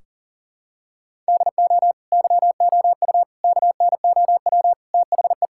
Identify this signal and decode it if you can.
Morse code, heard as a keyed tone, about 40 wpm, 700 Hz. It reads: DO YOU KNOW THE
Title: Sentences from Top 1600 Words 40wpm